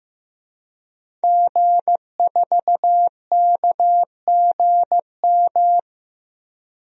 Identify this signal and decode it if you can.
Morse code, heard as a keyed tone, about 15 wpm, 700 Hz. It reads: G4KGM
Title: DXCC-100 Call Signs - 5 Characters - Repeated 2x - 15wpm